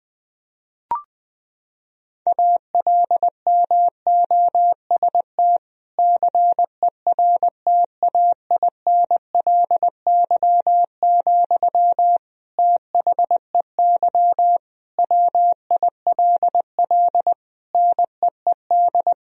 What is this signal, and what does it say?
Morse code, heard as a keyed tone, about 21 wpm, 700 Hz. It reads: ALMOST CERTAINLY, THEY WILL NEED
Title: Sentences from Top 1000 Words - Review Part 4 - 20wpm